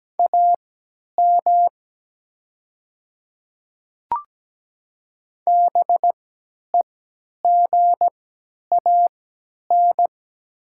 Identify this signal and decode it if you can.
Morse code, heard as a keyed tone, about 17 wpm, spaced as for 10 wpm, 700 Hz. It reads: AM BEGAN
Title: 3x Character Spacing - Top 500 Words - 17wpm